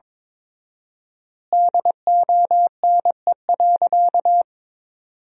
Morse code, heard as a keyed tone, about 22 wpm, 700 Hz. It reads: DONE.
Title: 4x Word-Spacing - Sentences from Top 500 Words - 22wpm